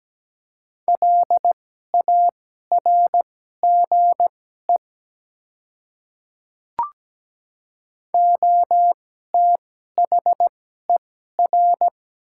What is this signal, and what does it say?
Morse code, heard as a keyed tone, about 17 wpm, spaced as for 12 wpm, 700 Hz. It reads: LARGE OTHER
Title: 2x Character Spacing - Top 300 Words - 17wpm